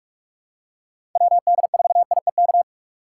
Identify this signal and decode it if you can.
Morse code, heard as a keyed tone, about 41 wpm, 700 Hz. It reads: WD4IEK
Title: USA Call Signs - 6 Characters - 3x Character Spacing - 45wpm